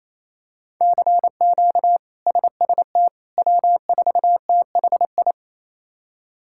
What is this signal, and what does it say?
Morse code, heard as a keyed tone, about 28 wpm, 700 Hz. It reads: CQ SST W4THI
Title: K1USN SST QSO Practice - 28wpm